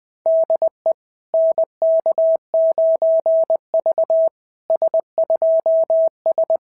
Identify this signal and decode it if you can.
Morse code, heard as a keyed tone, about 20 wpm, 690 Hz. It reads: DE NK9V S2S
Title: SOTA QSO Practice 20wpm